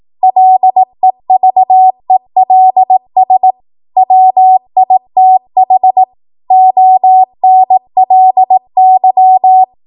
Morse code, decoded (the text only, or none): LEVELS WITH ONLY